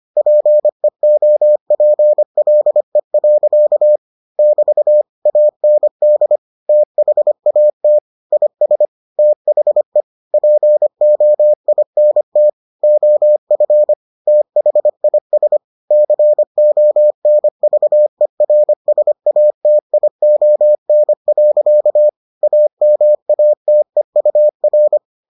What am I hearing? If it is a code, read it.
Morse code, heard as a keyed tone, about 25 wpm, 600 Hz. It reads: PEOPLE. <BT> AND THAT IS THE POINT OF THIS CONVERSATION. AMATEUR